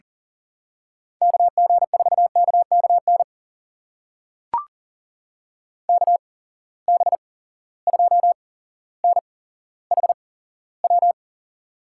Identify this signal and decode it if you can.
Morse code, heard as a keyed tone, about 39 wpm, spaced as for 17 wpm, 700 Hz. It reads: KG4KKN KB2NHW
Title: USA Call Signs - 6 Characters - 8x Character Spacing - 40wpm